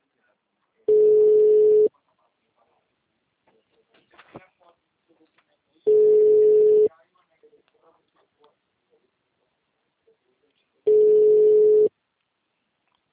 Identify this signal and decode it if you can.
Ringback tone (Europe, 425 Hz), with speech.